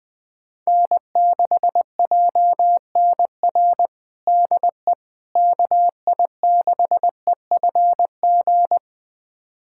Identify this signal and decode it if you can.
Morse code, heard as a keyed tone, about 20 wpm, 700 Hz. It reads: N6JNR DE KI6EFG
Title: USA Callsign DE Callsign 20wpm